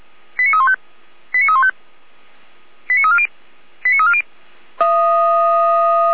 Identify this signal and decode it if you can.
Five-tone selective call (ZVEI), heard as quick runs of five tones, tones of about 70 ms. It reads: 89315 89315 89340 89340